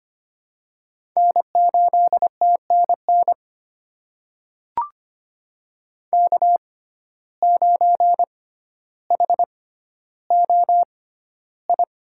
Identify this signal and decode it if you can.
Morse code, heard as a keyed tone, about 25 wpm, spaced as for 13 wpm, 700 Hz. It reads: N8TNN K9HOI
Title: USA Call Signs - 5 Characters - 6x Character Spacing - 25wpm